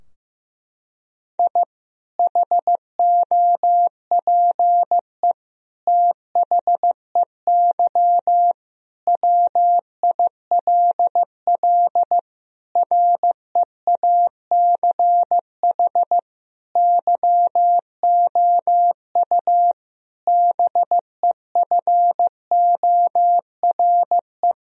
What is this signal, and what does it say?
Morse code, heard as a keyed tone, about 15 wpm, 700 Hz. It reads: I HOPE THEY WILL REACH YOU BEFORE
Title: Sentences from Top 1000 Words - Review Part 2 - 15wpm